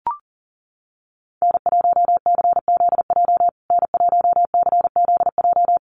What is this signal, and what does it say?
Morse code, heard as a keyed tone, about 40 wpm, 700 Hz. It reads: N1CZJ N1CZJ
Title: USA Call Signs Repeated 3x 40wpm